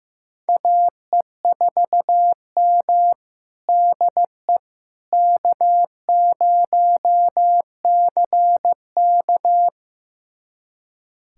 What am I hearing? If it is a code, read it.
Morse code, heard as a keyed tone, about 15 wpm, 700 Hz. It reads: AE4M DE K0CK
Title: USA Callsign DE Callsign - 4 Characters - 15wpm